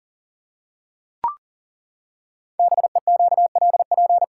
Morse code, heard as a keyed tone, about 40 wpm, 690 Hz. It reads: BEQLP
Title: Code Groups - 5 Letters - 40wpm